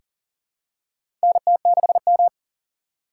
Morse code, heard as a keyed tone, about 40 wpm, 700 Hz. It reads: NT6M
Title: USA Call Signs - 3x Character Spacing - 40wpm